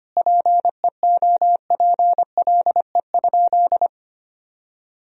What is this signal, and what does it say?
Morse code, heard as a keyed tone, about 25 wpm, 700 Hz. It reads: PEOPLE?
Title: Sentences from Top 1000 Words - Review Part 3 - 25wpm